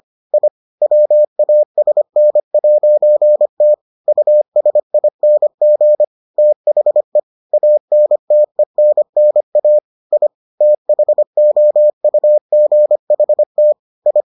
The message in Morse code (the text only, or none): I WASN'T USING THE ANTENNA I THOUGHT I